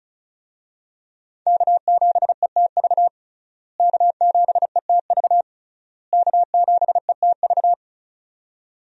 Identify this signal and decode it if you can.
Morse code, heard as a keyed tone, about 36 wpm, 700 Hz. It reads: K7ETV K7ETV K7ETV
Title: USA Call Signs - 5 Characters - Repeated 3x - 35wpm